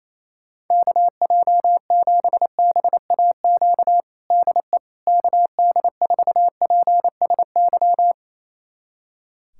Morse code, heard as a keyed tone, about 28 wpm, 700 Hz. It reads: KJ7BAQ DE KD4PSY
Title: USA Callsign DE Callsign 28wpm